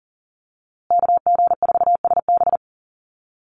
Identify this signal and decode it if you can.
Morse code, heard as a keyed tone, about 40 wpm, 700 Hz. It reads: KG4SB
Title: USA Call Signs Repeated 3x 40wpm